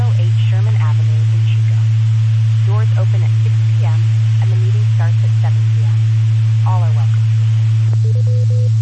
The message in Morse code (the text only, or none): none